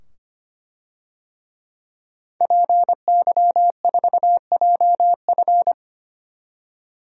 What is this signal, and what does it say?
Morse code, heard as a keyed tone, about 25 wpm, 700 Hz. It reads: PY4JF
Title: DXCC-100 Call Signs - 5 Characters - 25wpm